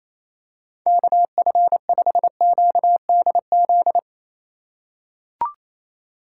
Morse code, heard as a keyed tone, about 28 wpm, 690 Hz. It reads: KF5QDZ
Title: USA Call Signs - 6 Characters - 4x Character Spacing - 28wpm